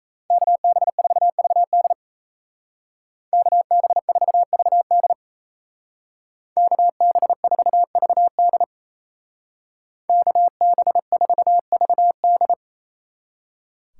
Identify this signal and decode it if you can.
Morse code, heard as a keyed tone, about 34 wpm, 700 Hz. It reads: KB4VD KB4VD KB4VD KB4VD
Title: Speed Racing - USA 2x2 Call Signs 28wpm